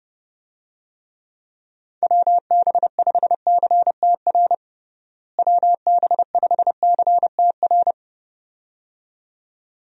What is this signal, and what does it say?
Morse code, heard as a keyed tone, about 30 wpm, 700 Hz. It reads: WB5CTR WB5CTR
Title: USA Call Signs - 6 Characters - Repeated 2x - 30wpm